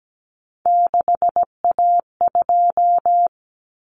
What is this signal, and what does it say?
Morse code, heard as a keyed tone, about 17 wpm, 700 Hz. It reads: 6A2
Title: Common Cut Numbers - 3-Digit 17wpm